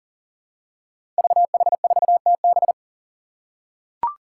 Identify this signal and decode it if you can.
Morse code, heard as a keyed tone, about 38 wpm, 700 Hz. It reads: VH4TB
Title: Five Challenging Letter-Number Combinations 40wpm